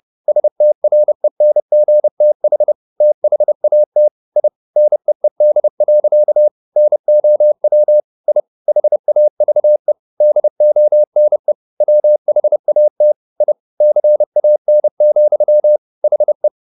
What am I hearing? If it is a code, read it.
Morse code, heard as a keyed tone, about 30 wpm, 600 Hz. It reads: STRENGTH THAT I NEED. NOW I HAVE DONE WHAT I CAN, HE